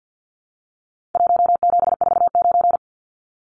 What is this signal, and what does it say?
Morse code, heard as a keyed tone, about 50 wpm, 700 Hz. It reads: 1749
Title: Numbers - Contest Style - 4-Digit 50wpm